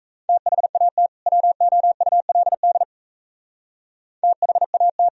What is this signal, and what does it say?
Morse code, heard as a keyed tone, about 35 wpm, 700 Hz. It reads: THAT WOULD THAT
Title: Speed-Racing - Sets of 2 Words 28wpm